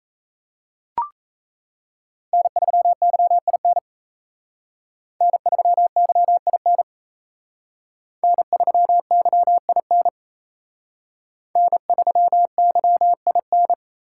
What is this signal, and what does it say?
Morse code, heard as a keyed tone, about 35 wpm, 700 Hz. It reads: N3YIN N3YIN N3YIN N3YIN
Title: Speed Racing - USA 1x3 Call Signs 28wpm